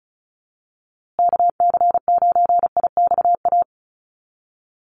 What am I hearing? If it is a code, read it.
Morse code, heard as a keyed tone, about 34 wpm, 700 Hz. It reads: KC9IXA
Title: USA Call Signs Repeated 3x 35wpm